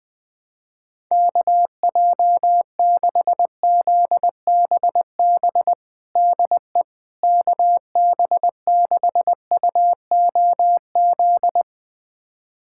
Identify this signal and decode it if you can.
Morse code, heard as a keyed tone, about 20 wpm, 700 Hz. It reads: KJ6ZBB DE KB6UOZ
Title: USA Callsign DE Callsign 20wpm